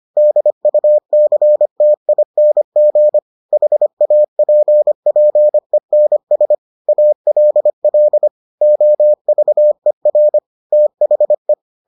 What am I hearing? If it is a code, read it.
Morse code, heard as a keyed tone, about 25 wpm, 600 Hz. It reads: DUCTING HAPPENS ALL OVER THE